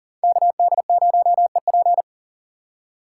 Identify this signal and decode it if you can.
Morse code, heard as a keyed tone, about 42 wpm, 700 Hz. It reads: KD0EP
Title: Speed Racing - USA 2x2 Call Signs 40wpm